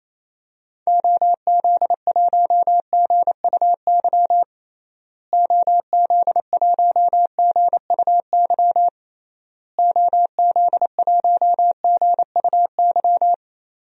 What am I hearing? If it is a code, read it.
Morse code, heard as a keyed tone, about 28 wpm, 700 Hz. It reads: OZ1GUY OZ1GUY OZ1GUY
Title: DXCC-100 Call Signs - 6 Characters - Repeated 3x - 28wpm